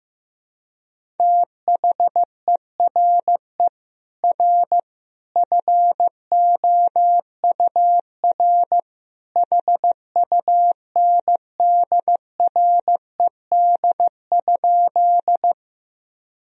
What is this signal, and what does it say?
Morse code, heard as a keyed tone, about 16 wpm, 700 Hz. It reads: THERE R FOUR HUNDRED?
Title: Sentences from Top 400 Words and Abbreviations 15wpm